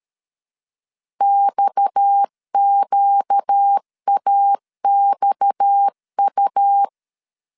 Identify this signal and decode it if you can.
Morse code, heard as a keyed tone, about 13 wpm, 800 Hz. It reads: XQAXU
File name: LEZIONI / 6 PKX0 / 6 PKX 12wpm 2024